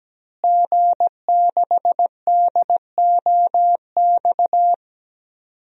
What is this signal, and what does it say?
Morse code, heard as a keyed tone, about 17 wpm, 700 Hz. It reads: G6DOX
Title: DXCC-100 Call Signs - 5 Characters - Repeated 3x - 17wpm